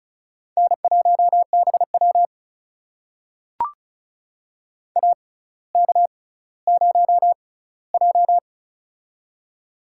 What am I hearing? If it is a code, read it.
Morse code, heard as a keyed tone, about 36 wpm, spaced as for 18 wpm, 700 Hz. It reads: N1BW AK0J
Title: USA Call Signs - 4 Characters - 6x Character Spacing - 35wpm